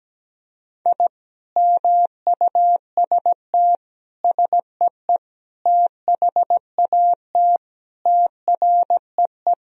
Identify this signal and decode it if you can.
Morse code, heard as a keyed tone, about 17 wpm, 700 Hz. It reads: I MUST SEE THAT TREE
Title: Sentences from Top 1000 Words - Review Part 1 - 17wpm